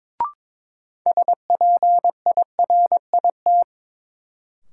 Morse code, heard as a keyed tone, about 22 wpm, 700 Hz. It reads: SPIRIT
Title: Sentences from Top 1400 Words - New Word Drill - 22wpm